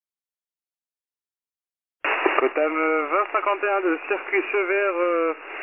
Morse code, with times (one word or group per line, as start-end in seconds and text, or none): none